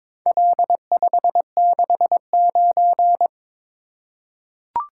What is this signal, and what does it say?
Morse code, heard as a keyed tone, about 22 wpm, 700 Hz. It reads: L569
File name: practice / 22wpm / Four Letter-Number Combinations - 22wpm